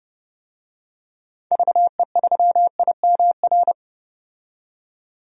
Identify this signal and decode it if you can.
Morse code, heard as a keyed tone, about 32 wpm, 700 Hz. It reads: VE3IMR
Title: DXCC-100 Call Signs - 6 Characters - 30wpm